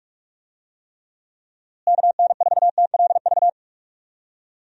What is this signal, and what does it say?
Morse code, heard as a keyed tone, about 43 wpm, 700 Hz. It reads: KN4TLV
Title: USA Call Signs - 6 Characters - Repeated 2x - 45wpm